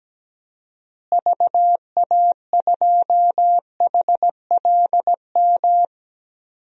Morse code, heard as a keyed tone, about 17 wpm, 700 Hz. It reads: VA2HLM
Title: DXCC-100 Call Signs - 6 Characters - Repeated 3x - 17wpm